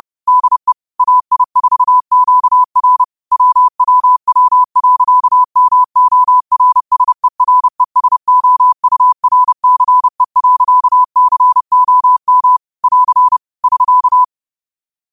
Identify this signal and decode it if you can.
Morse code, heard as a keyed tone, about 31 wpm, 1000 Hz. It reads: DE AI4QR WWW.MORSERESOURCE.COM <AR> <SK>
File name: qotdfiles / 30wpm